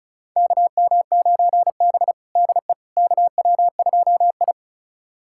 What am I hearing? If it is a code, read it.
Morse code, heard as a keyed tone, about 36 wpm, 700 Hz. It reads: KM9B DE KW2I
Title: USA Callsign DE Callsign - 4 Characters - 35wpm